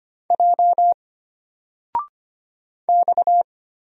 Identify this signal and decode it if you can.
Morse code, heard as a keyed tone, about 25 wpm, 700 Hz. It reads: J X
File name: Single-Letter-Number-25wpm